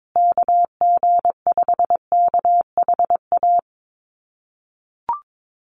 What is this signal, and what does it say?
Morse code, heard as a keyed tone, about 22 wpm, 700 Hz. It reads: KG5KHA
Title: USA Call Signs Repeated 3x 22wpm